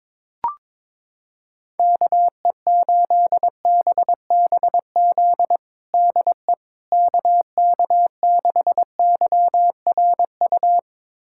Morse code, heard as a keyed tone, about 22 wpm, 700 Hz. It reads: KE8BBZ DE KK6YRU
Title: USA Callsign DE Callsign - 6 Characters - 22wpm